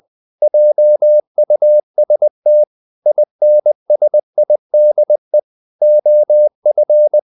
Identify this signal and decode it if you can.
Morse code, heard as a keyed tone, about 20 wpm, 600 Hz. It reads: JUST INSIDE OF
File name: the old man and the sea / the old man and the sea part 1 20wpm